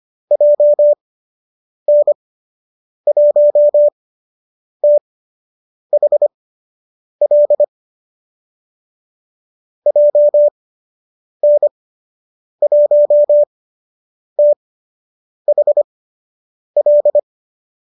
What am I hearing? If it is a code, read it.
Morse code, heard as a keyed tone, about 25 wpm, spaced as for 8 wpm, 590 Hz. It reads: JN1THL JN1THL
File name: qso003 08